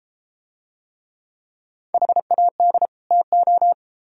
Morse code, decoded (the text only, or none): HAD TO